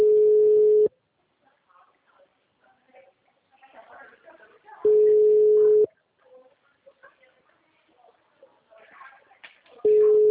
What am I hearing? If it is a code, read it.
Ringback tone (Europe, 425 Hz).